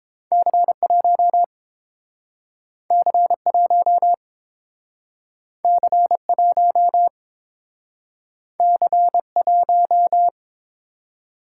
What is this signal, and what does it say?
Morse code, heard as a keyed tone, about 30 wpm, 700 Hz. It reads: C1 C1 C1 C1